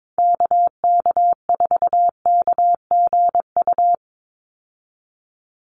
Morse code, heard as a keyed tone, about 22 wpm, 700 Hz. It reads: KK4KGU
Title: USA Call Signs 22wpm